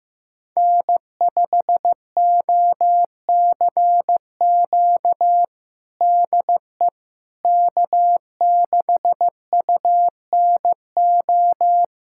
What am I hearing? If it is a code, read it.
Morse code, heard as a keyed tone, about 16 wpm, 700 Hz. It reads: N5OCQ DE K6UNO
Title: USA Callsign DE Callsign - 5 Characters - 15wpm